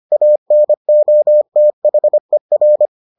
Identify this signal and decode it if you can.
Morse code, heard as a keyed tone, about 25 wpm, 600 Hz. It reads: ANOTHER